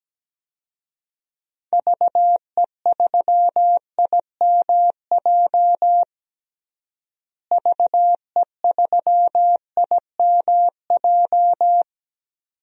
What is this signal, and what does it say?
Morse code, heard as a keyed tone, about 17 wpm, 700 Hz. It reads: VE3IMJ VE3IMJ
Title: DXCC-100 Call Signs - 6 Characters - Repeated 2x - 17wpm